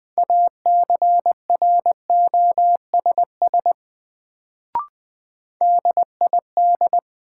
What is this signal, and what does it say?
Morse code, heard as a keyed tone, about 20 wpm, 700 Hz. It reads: ACROSS DID
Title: Sets of 2 Words - New Word Drill - 20wpm